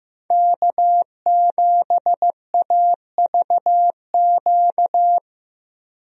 Morse code, heard as a keyed tone, about 15 wpm, 700 Hz. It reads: K7AVQ
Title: USA Call Signs - 5 Characters - 15wpm